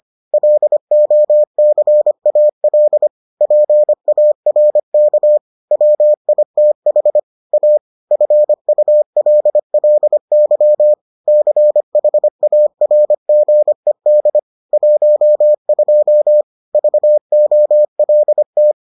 Morse code, heard as a keyed tone, about 25 wpm, 600 Hz. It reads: LOCAL PARK WITH A FULLY CHARGED 12 VOLT